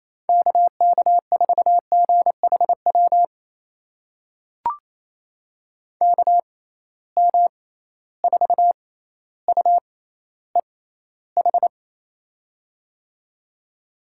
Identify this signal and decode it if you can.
Morse code, heard as a keyed tone, about 28 wpm, spaced as for 14 wpm, 700 Hz. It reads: KK4GHW KM4UEH
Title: USA Call Signs - 6 Characters - 6x Character Spacing - 28wpm